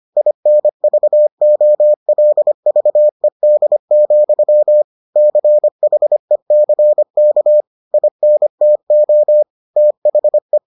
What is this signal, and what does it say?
Morse code, heard as a keyed tone, about 25 wpm, 600 Hz. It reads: INVOLVED, CHECK INTO THE